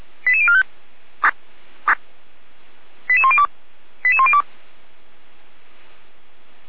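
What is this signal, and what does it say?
Five-tone selective call (ZVEI), heard as quick runs of five tones, tones of about 70 ms. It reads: 88045 88102 88102
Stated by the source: ZVEI five-tone call 88109 received by SDR.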